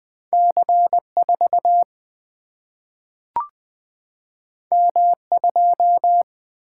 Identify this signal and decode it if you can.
Morse code, heard as a keyed tone, about 20 wpm, 700 Hz. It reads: C 4 M 2
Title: Single-Letter Single-Number 20wpm